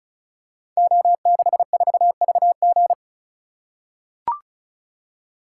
Morse code, heard as a keyed tone, about 36 wpm, 700 Hz. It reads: O64VG
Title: Code Groups - 5 Letter:Numbers - 35wpm